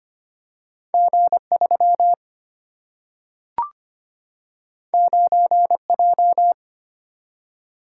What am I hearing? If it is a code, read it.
Morse code, heard as a keyed tone, about 25 wpm, 700 Hz. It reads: G3 9J
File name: Two Letter-Number Combinations - 25wpm